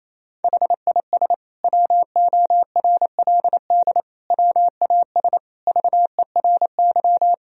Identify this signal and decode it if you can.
Morse code, heard as a keyed tone, about 28 wpm, 700 Hz. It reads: HIS WORLD WAS VERY
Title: Sentences from Top 1000 Words - Review Part 1 - 28wpm